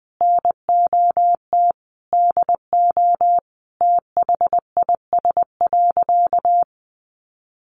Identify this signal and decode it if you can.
Morse code, heard as a keyed tone, about 19 wpm, 700 Hz. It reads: NOT DO THIS.
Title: Sentences from Top 1100 Words 20wpm